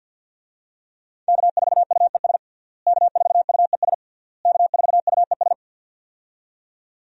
Morse code, heard as a keyed tone, about 46 wpm, 700 Hz. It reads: K4UES K4UES K4UES
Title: USA Call Signs - 5 Characters - Repeated 3x - 50wpm